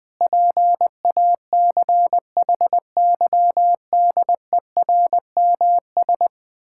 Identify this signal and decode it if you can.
Morse code, heard as a keyed tone, about 20 wpm, 700 Hz. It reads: PACHYDERMS
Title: Pangram Challenge v2 - 20wpm